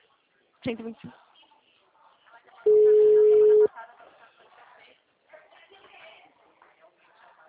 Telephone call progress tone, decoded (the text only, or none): ringback tone